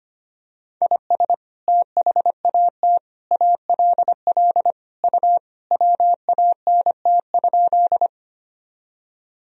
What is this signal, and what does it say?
Morse code, heard as a keyed tone, about 25 wpm, 700 Hz. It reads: IS THAT ALL U WANT?
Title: Sentences from Top 200 Words and Abbreviations 25wpm